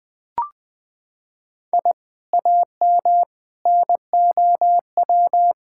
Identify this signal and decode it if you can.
Morse code, heard as a keyed tone, about 20 wpm, 700 Hz. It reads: I AM NOW
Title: Sentences from Top 1000 Words - Review Part 6 - 20wpm